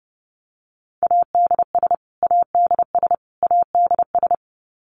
Morse code, heard as a keyed tone, about 30 wpm, 700 Hz. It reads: ADS ADS ADS
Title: Abbreviations Repeated 3x - 30wpm